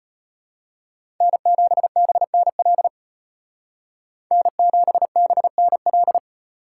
Morse code, heard as a keyed tone, about 33 wpm, 700 Hz. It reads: N7BNL N7BNL